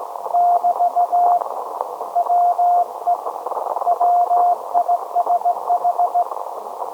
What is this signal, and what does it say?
Morse code, heard as a keyed tone, about 17 wpm, 710 Hz. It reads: <BT> WE WISH
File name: storage / message Grimeton SAQ